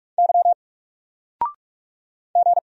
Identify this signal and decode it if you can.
Morse code, heard as a keyed tone, about 44 wpm, 700 Hz. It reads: Y G
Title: Single Character (Letter:Number:Punctuation:Prosign) 45wpm